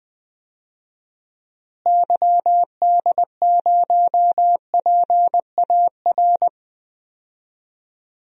Morse code, heard as a keyed tone, about 20 wpm, 700 Hz. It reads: YD0PAR
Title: DXCC-100 Call Signs - 6 Characters - Repeated 2x - 20wpm